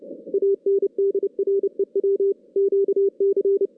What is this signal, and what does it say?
Morse code, heard as a keyed tone, about 29 wpm, 410 Hz. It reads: ANDREW QC
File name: 2016 JAN CW NAQP / 2016 Jan naqp k5ot - 0-6